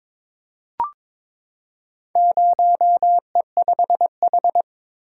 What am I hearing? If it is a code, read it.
Morse code, heard as a keyed tone, about 22 wpm, 700 Hz. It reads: 0E5H
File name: practice / 22wpm / Four Letter-Number Combinations - 22wpm